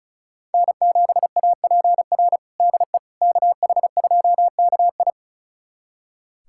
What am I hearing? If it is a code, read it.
Morse code, heard as a keyed tone, about 34 wpm, 700 Hz. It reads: N7APR DE KH2KI
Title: USA Callsign DE Callsign - 5 Characters - 35wpm